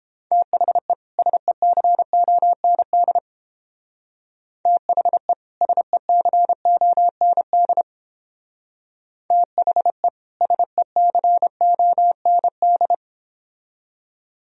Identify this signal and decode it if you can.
Morse code, heard as a keyed tone, about 31 wpm, 700 Hz. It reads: THE SECOND THE SECOND THE SECOND